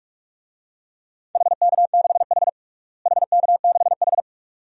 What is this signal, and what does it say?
Morse code, heard as a keyed tone, about 42 wpm, 700 Hz. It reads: HK6H HK6H
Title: DXCC-100 Call Signs - 4 Characters - Repeated 2x - 45wpm